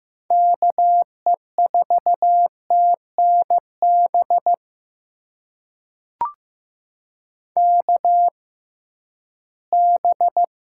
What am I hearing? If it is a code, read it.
Morse code, heard as a keyed tone, about 15 wpm, spaced as for 11 wpm, 700 Hz. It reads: KE4TNB KB
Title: USA Call Signs - 6 Characters - 6x Character Spacing - 15wpm